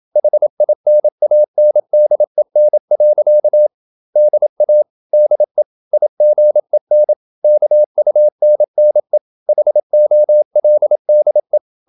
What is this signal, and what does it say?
Morse code, heard as a keyed tone, about 28 wpm, 600 Hz. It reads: HINANDEN. DA DE IGEN KUNNE HOLDE
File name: Ferie QRQ 27wpm